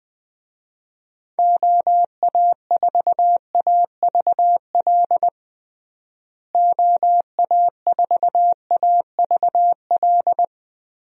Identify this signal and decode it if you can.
Morse code, heard as a keyed tone, about 20 wpm, 700 Hz. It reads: OA4AVL OA4AVL
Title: USA Call Signs - 6 Characters - Repeated 3x - 20wpm